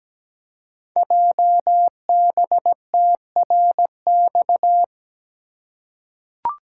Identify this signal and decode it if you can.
Morse code, heard as a keyed tone, about 17 wpm, 700 Hz. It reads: JBTRX
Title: Code Groups - 5 Letters - 17wpm